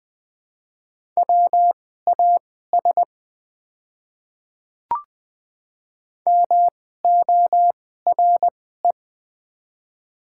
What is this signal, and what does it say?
Morse code, heard as a keyed tone, about 20 wpm, spaced as for 14 wpm, 700 Hz. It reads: WAS MORE
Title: 2x Character Spacing - Top 100 Words - 20wpm